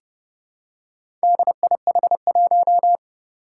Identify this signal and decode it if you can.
Morse code, heard as a keyed tone, about 29 wpm, 700 Hz. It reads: DIH1
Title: Four Challenging Letter-Number Combinations 30wpm